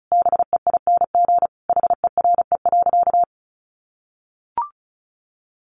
Morse code, heard as a keyed tone, about 34 wpm, 700 Hz. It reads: BEING HERE.
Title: Sentences from Top 700 Words 35wpm